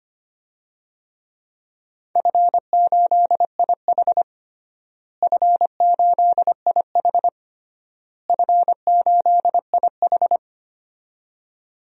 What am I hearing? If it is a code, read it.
Morse code, heard as a keyed tone, about 25 wpm, 700 Hz. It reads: F8IH F8IH F8IH
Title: DXCC-100 Call Signs - 4 Characters - Repeated 3x - 25wpm